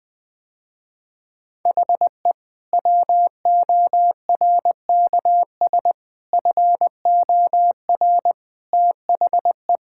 Morse code, heard as a keyed tone, about 20 wpm, 700 Hz. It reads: HE WORKS FOR THE
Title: Sentences with 2-Word Phrases - Spoken Beginning - 20wpm